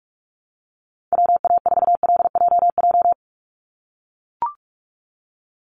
Morse code, heard as a keyed tone, about 44 wpm, 700 Hz. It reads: WA4LJJ
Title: Call Signs - 45wpm